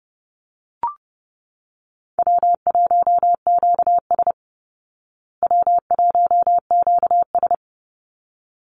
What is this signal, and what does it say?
Morse code, heard as a keyed tone, about 30 wpm, 700 Hz. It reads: W1QS W1QS
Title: Call Signs Repeated 3x - 30wpm